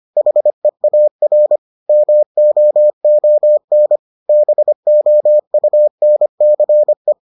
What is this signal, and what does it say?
Morse code, heard as a keyed tone, about 25 wpm, 600 Hz. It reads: HEAR MOON BOUNCE